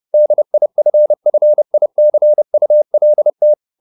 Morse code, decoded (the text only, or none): DIFFICULT